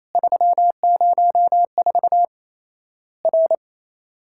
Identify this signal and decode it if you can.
Morse code, heard as a keyed tone, about 28 wpm, 700 Hz. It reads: 304 R
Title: Worked All Europe DX Contest - QTC Exchange - 28wpm